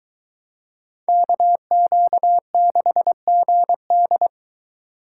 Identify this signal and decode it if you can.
Morse code, heard as a keyed tone, about 22 wpm, 700 Hz. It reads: KQ6GD